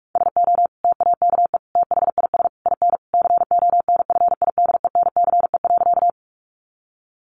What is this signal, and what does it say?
Morse code, heard as a keyed tone, about 44 wpm, 700 Hz. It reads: SO TAKE THIS IN CONFIDENCE.
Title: Sentences from Top 1500 Words 45wpm